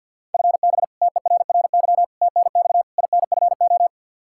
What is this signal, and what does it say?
Morse code, heard as a keyed tone, about 51 wpm, 700 Hz. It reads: FB TERRY TNX INFO
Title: Practice QSO - Exchange 3 - More Info - 50wpm